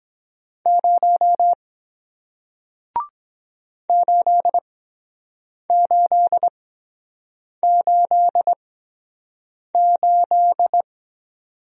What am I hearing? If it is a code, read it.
Morse code, heard as a keyed tone, about 23 wpm, spaced as for 7 wpm, 700 Hz. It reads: 0 8888